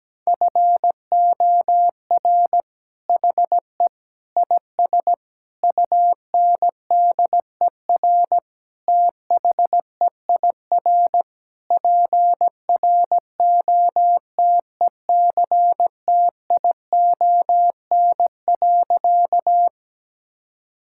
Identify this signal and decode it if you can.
Morse code, heard as a keyed tone, about 17 wpm, 700 Hz. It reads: FOR HE IS UNDER THEIR PROTECTION.
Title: Sentences from Top 1600 Words 17wpm